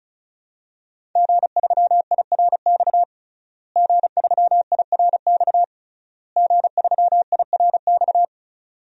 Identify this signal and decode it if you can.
Morse code, heard as a keyed tone, about 35 wpm, 700 Hz. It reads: G3IRX G3IRX G3IRX
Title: DXCC-100 Call Signs - 5 Characters - Repeated 3x - 35wpm